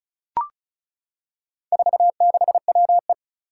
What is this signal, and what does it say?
Morse code, heard as a keyed tone, about 36 wpm, 700 Hz. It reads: 46WE
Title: Four Challenging Letter-Number Combinations 35wpm